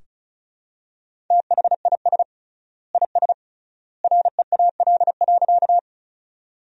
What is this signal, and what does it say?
Morse code, heard as a keyed tone, about 36 wpm, 700 Hz. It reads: THIS IS REAL.
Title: 3x Word-Spacing - Sentences from Top 300 Words - 35wpm